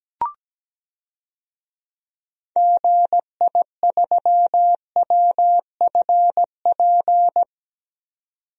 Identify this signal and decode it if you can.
Morse code, heard as a keyed tone, about 17 wpm, 700 Hz. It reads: GI3WFP
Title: DXCC-100 Call Signs - 6 Characters - 17wpm